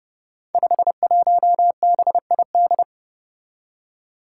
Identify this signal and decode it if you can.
Morse code, heard as a keyed tone, about 30 wpm, 700 Hz. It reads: 51BID
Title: Five Challenging Letter-Number Combinations 30wpm